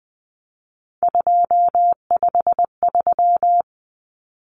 Morse code, heard as a keyed tone, about 20 wpm, 700 Hz. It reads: 253
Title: Numbers - 3-Digit 20wpm